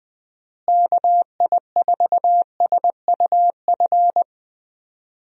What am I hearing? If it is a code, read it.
Morse code, heard as a keyed tone, about 20 wpm, 700 Hz. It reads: KI4SUF
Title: USA Call Signs - 6 Characters - Repeated 2x - 20wpm